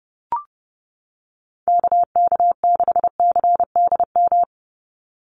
Morse code, heard as a keyed tone, about 30 wpm, 700 Hz. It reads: KK6CDM
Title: USA Call Signs 30wpm